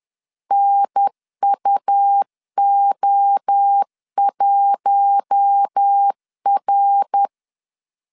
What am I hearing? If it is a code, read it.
Morse code, heard as a keyed tone, about 10 wpm, 800 Hz. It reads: NUO1R
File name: CW / DET / 1 6 1500C S B 10wpm 4137